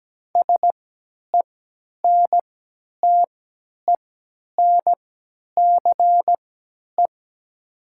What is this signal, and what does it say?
Morse code, heard as a keyed tone, about 17 wpm, spaced as for 10 wpm, 690 Hz. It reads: SENTENCE
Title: 3x Character Spacing - Top 300 Words - 17wpm